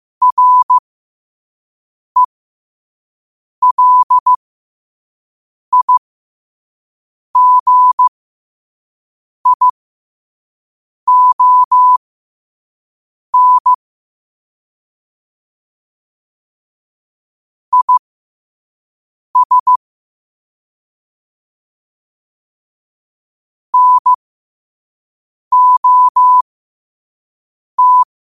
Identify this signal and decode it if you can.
Morse code, heard as a keyed tone, about 15 wpm, spaced as for 5 wpm, 1000 Hz. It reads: RELIGION IS NOT
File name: qotdfiles / 5wpm